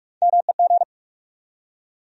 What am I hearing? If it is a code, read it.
Morse code, heard as a keyed tone, about 46 wpm, 700 Hz. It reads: MEG
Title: USA Common Names - Max 3 Letters - 45wpm